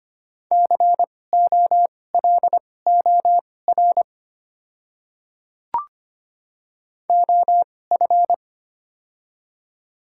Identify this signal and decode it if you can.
Morse code, heard as a keyed tone, about 25 wpm, spaced as for 18 wpm, 700 Hz. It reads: COLOR OF
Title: 2x Character Spacing - Top 500 Words - 25wpm